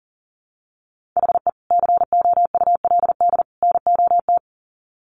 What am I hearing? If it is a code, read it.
Morse code, heard as a keyed tone, about 40 wpm, 700 Hz. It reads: HE COULD NOT
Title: Sets of 3 Words 40wpm